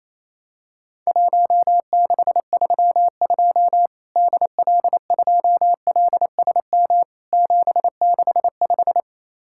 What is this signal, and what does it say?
Morse code, heard as a keyed tone, about 28 wpm, 700 Hz. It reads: 1632 DL2LSM 765
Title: Worked All Europe DX Contest - QTC Exchange - 28wpm